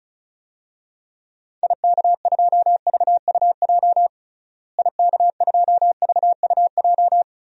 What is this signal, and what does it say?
Morse code, heard as a keyed tone, about 36 wpm, 700 Hz. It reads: IK2VUJ IK2VUJ
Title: DXCC-100 Call Signs - 6 Characters - Repeated 3x - 35wpm